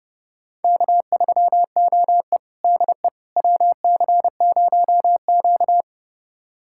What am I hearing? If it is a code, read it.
Morse code, heard as a keyed tone, about 30 wpm, 700 Hz. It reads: K3OE DE WC0Q
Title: USA Callsign DE Callsign - 4 Characters - 30wpm